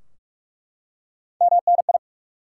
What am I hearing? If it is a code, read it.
Morse code, heard as a keyed tone, about 41 wpm, 700 Hz. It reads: MNI
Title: Top 90 QSO Elements 45wpm